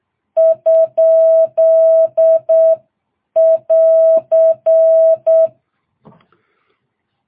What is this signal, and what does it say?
Morse code, heard as a keyed tone, about 8 wpm, 660 Hz. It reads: ? <AR>